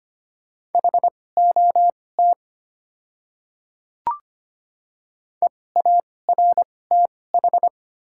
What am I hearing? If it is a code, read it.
Morse code, heard as a keyed tone, about 25 wpm, spaced as for 18 wpm, 700 Hz. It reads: HOT EARTH
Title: 2x Character Spacing - Top 200 Words - 25wpm